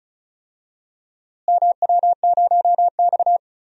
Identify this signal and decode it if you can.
Morse code, heard as a keyed tone, about 36 wpm, 700 Hz. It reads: MW0X
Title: DXCC-100 Call Signs - 4 Characters - Repeated 2x - 35wpm